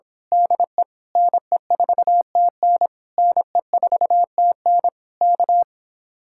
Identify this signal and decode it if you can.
Morse code, heard as a keyed tone, about 26 wpm, 700 Hz. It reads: DE NE4TN NE4TN K
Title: Speed-Racing Common QSO Phrases - 22wpm